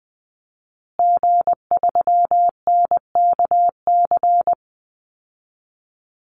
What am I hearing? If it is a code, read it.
Morse code, heard as a keyed tone, about 20 wpm, 700 Hz. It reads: G3NKC
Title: Call Signs Repeated 3x - 20wpm